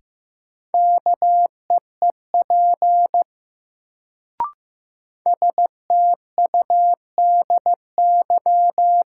Morse code, heard as a keyed tone, about 15 wpm, 700 Hz. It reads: KEEP STUDY
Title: Sentences from Top 300 Words - New Word Drill - 15wpm